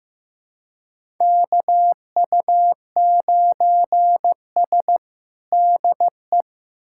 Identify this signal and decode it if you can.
Morse code, heard as a keyed tone, about 15 wpm, 700 Hz. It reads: KU9S DE
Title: USA Callsign DE Callsign - 4 Characters - 15wpm